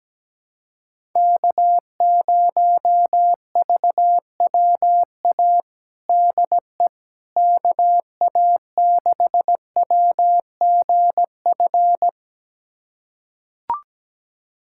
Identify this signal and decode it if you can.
Morse code, heard as a keyed tone, about 17 wpm, 700 Hz. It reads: K0VWA DE KA6WGF
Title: USA Callsign DE Callsign 17wpm